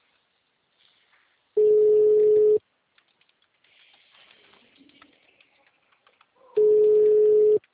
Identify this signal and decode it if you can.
Ringback tone (Europe, 425 Hz).
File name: out- -2015- - - .675